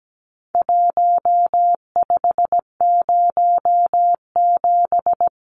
Morse code, heard as a keyed tone, about 17 wpm, 700 Hz. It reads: 1507
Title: Numbers - Contest Style - 4-Digit 17wpm